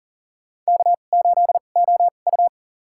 Morse code, heard as a keyed tone, about 40 wpm, spaced as for 29 wpm, 700 Hz. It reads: K8OU
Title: USA Call Signs - 4 Characters - 2x Character Spacing - 40wpm